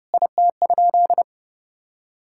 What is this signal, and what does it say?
Morse code, heard as a keyed tone, about 30 wpm, 700 Hz. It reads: IT?
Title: Sentences from Top 1600 Words 30wpm